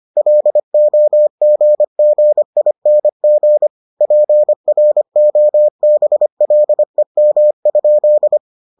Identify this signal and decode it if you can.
Morse code, heard as a keyed tone, about 25 wpm, 600 Hz. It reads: LOGGING PROBLEM?